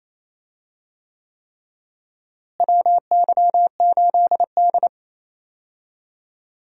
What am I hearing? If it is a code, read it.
Morse code, heard as a keyed tone, about 28 wpm, 700 Hz. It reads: WY8D
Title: USA Call Signs - 4 Characters - Repeated 3x - 28wpm